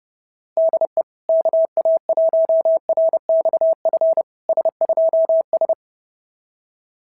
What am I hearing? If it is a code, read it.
Morse code, heard as a keyed tone, about 30 wpm, 690 Hz. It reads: DE KA1RXF S2S
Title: SOTA QSO Practice 30wpm